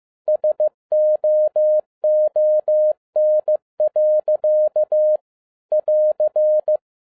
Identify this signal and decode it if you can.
Morse code, heard as a keyed tone, about 15 wpm, 610 Hz. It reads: SOON. <AR>
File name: 2011-10-mp3 / n12-JL3YMV 15wpm